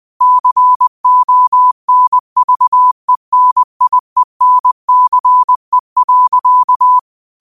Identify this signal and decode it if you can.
Morse code, heard as a keyed tone, about 20 wpm, 1000 Hz. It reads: CONVENIENCE.
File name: qotdfiles / 20wpm